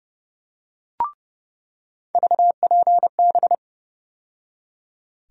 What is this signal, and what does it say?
Morse code, heard as a keyed tone, about 30 wpm, 700 Hz. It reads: VPB
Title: USA Call Sign Postfixes - 3 Letters 30wpm